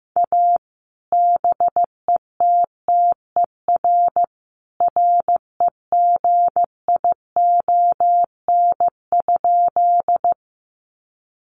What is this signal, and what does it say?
Morse code, heard as a keyed tone, about 15 wpm, 700 Hz. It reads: A BETTER REGION?
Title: Sentences from Top 1500 Words 15wpm